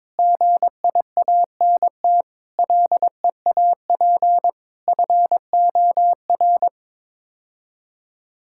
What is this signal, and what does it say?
Morse code, heard as a keyed tone, about 22 wpm, 700 Hz. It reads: GIANT LEAP FOR
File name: Sets of 9 Words - 22wpm